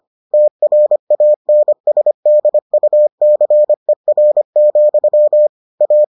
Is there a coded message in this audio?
Yes: Morse code from 0.3 to 6.0 s, heard as a keyed tone, about 25 wpm, 600 Hz. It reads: TRANSDUCER, A